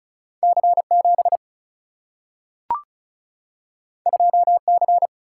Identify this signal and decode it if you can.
Morse code, heard as a keyed tone, about 36 wpm, 700 Hz. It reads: C7 2C
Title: Single-Letter Single-Number 35wpm